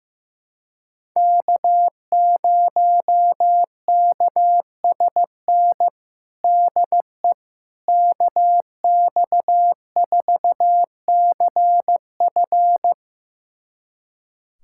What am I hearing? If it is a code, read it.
Morse code, heard as a keyed tone, about 15 wpm, 700 Hz. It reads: K0KSN DE KX4CF
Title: USA Callsign DE Callsign 15wpm